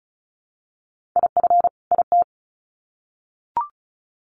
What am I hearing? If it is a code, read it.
Morse code, heard as a keyed tone, about 34 wpm, 700 Hz. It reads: IF IT
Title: Sets of 2 Words 35wpm